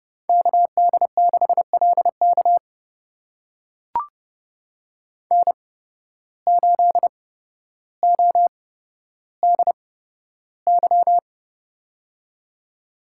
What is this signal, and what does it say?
Morse code, heard as a keyed tone, about 32 wpm, spaced as for 13 wpm, 700 Hz. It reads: KD6LK N8ODY
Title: USA Call Signs - 5 Characters - 8x Character Spacing - 30wpm